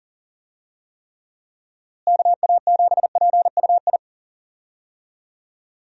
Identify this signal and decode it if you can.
Morse code, heard as a keyed tone, about 40 wpm, 700 Hz. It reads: KA7PUI
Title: USA Call Signs - 6 Characters - Repeated 3x - 40wpm